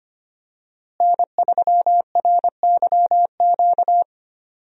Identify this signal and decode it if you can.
Morse code, heard as a keyed tone, about 25 wpm, 700 Hz. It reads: N3RYQ
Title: USA Call Signs - 5 Characters - Repeated 2x - 25wpm